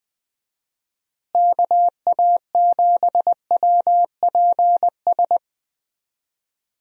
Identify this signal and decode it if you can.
Morse code, heard as a keyed tone, about 20 wpm, 700 Hz. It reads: KA7WPS
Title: USA Call Signs - 6 Characters - Repeated 2x - 20wpm